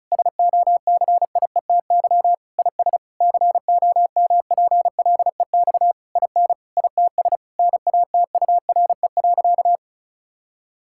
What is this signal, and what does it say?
Morse code, heard as a keyed tone, about 36 wpm, 700 Hz. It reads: SOCIETY IS COMPLEX IN ITS NATURE.
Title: Sentences from Top 1000 Words - Review Part 6 - 35wpm